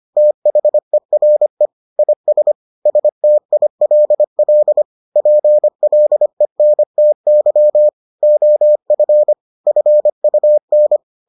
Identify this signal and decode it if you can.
Morse code, heard as a keyed tone, about 25 wpm, 600 Hz. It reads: THERE IS STILL PLENTY OF FUN